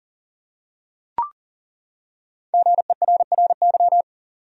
Morse code, heard as a keyed tone, about 40 wpm, 700 Hz. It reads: GERRY
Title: USA Common Names - 40wpm